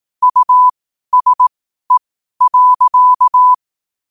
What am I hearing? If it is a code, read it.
Morse code, heard as a keyed tone, about 18 wpm, spaced as for 12 wpm, 1000 Hz. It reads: USE.
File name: qotdfiles / 12wpm